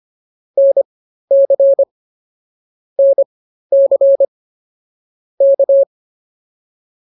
Morse code, heard as a keyed tone, about 24 wpm, spaced as for 13 wpm, 550 Hz. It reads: NC NC K